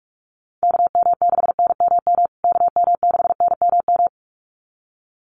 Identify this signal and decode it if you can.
Morse code, heard as a keyed tone, about 44 wpm, 700 Hz. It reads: KM6NMM KM6NMM
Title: USA Call Signs Repeated 2x 45wpm